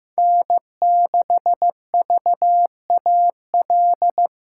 Morse code, heard as a keyed tone, about 15 wpm, 700 Hz. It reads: N6VAL
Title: USA Callsign DE Callsign - 5 Characters - 15wpm